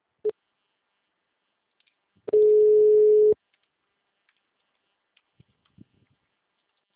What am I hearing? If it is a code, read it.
Ringback tone (Europe, 425 Hz).